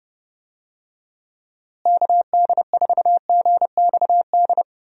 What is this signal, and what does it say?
Morse code, heard as a keyed tone, about 28 wpm, 700 Hz. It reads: KD4GXD
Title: USA Call Signs - 6 Characters - Repeated 2x - 30wpm